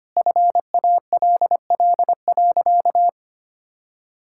Morse code, heard as a keyed tone, about 25 wpm, 700 Hz. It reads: FALL.
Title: Sentences from Top 1000 Words - Review Part 6 - 25wpm